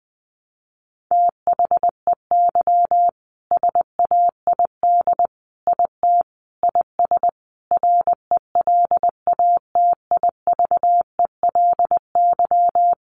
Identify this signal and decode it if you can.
Morse code, heard as a keyed tone, about 19 wpm, 700 Hz. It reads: THEY SAID IT IS RELATIVELY
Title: Sentences from Top 1100 Words 20wpm